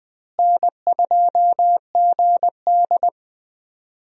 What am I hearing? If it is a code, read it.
Morse code, heard as a keyed tone, about 20 wpm, 700 Hz. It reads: N2GD
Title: USA Call Signs - 4 Characters - Repeated 2x - 20wpm